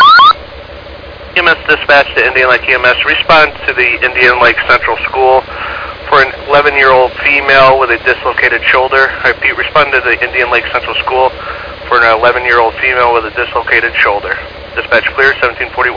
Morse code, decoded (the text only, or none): none